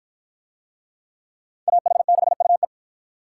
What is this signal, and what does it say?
Morse code, heard as a keyed tone, about 49 wpm, 700 Hz. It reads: AH6UE